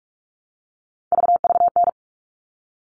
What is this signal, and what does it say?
Morse code, heard as a keyed tone, about 44 wpm, 700 Hz. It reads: VVN
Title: All Cut Numbers - 3-Digit 45wpm